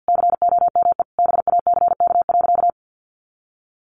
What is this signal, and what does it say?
Morse code, heard as a keyed tone, about 50 wpm, 700 Hz. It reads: COME BACK.
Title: Sentences from Top 1300 Words 50wpm